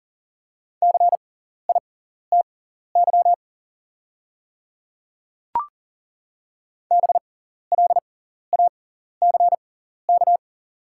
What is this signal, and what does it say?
Morse code, heard as a keyed tone, about 40 wpm, spaced as for 14 wpm, 690 Hz. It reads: CITY BLACK
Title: 6x Character Spacing - Top 500 Words - 40wpm